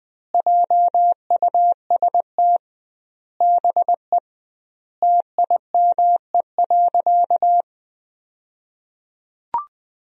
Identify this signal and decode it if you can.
Morse code, heard as a keyed tone, about 21 wpm, 700 Hz. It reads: JUST BE TIME.
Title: 2x Word-Spacing - Sentences from Top 200 Words - 20wpm